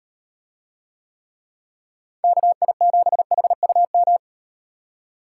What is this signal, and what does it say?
Morse code, heard as a keyed tone, about 38 wpm, 700 Hz. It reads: KI7HUM